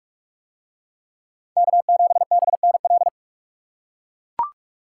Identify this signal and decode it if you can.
Morse code, heard as a keyed tone, about 43 wpm, 700 Hz. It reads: K7BNL